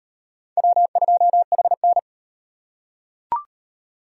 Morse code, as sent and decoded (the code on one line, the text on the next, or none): .-- ..--- .... -.
W2HN